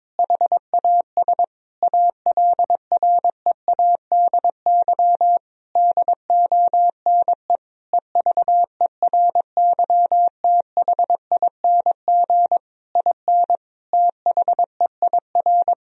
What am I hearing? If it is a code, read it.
Morse code, heard as a keyed tone, about 22 wpm, 700 Hz. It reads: HAS ALREADY DONE EVERYTHING IN THEIR
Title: Sentences from Top 1000 Words - Review Part 2 - 22wpm